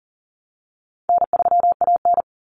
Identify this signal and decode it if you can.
Morse code, heard as a keyed tone, about 40 wpm, 700 Hz. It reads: N3AN
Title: Common Cut Numbers - 4-Digit 40wpm